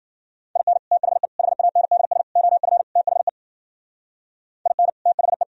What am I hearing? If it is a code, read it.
Morse code, heard as a keyed tone, about 51 wpm, 700 Hz. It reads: IN THE HANDS OF THE IN THE